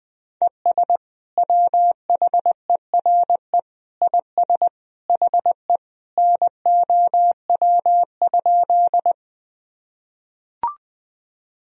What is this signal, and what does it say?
Morse code, heard as a keyed tone, about 20 wpm, 700 Hz. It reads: ES WHERE IS HE NOW?
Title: Sentences from Top 200 Words and Abbreviations 20wpm